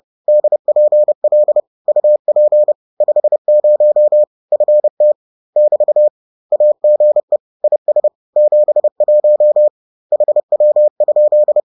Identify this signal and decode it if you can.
Morse code, heard as a keyed tone, about 30 wpm, 600 Hz. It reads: DPL UP 50 FT <BT> AGE IS 71 HW?